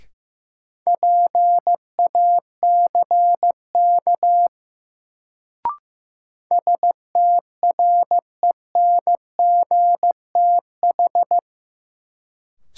Morse code, heard as a keyed tone, about 15 wpm, 700 Hz. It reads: PACK STRENGTH
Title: Sentences from Top 1300 Words - New Word Drill Alt - 15wpm